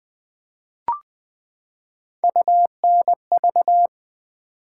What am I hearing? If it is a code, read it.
Morse code, heard as a keyed tone, about 20 wpm, 700 Hz. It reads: UNV
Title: Build A Word - 3 Letters At A Time - 6-Letter Words - 20wpm